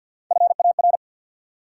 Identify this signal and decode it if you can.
Morse code, heard as a keyed tone, about 52 wpm, 700 Hz. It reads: FAR
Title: Sentences from Top 300 Words - New Word Drill - 50wpm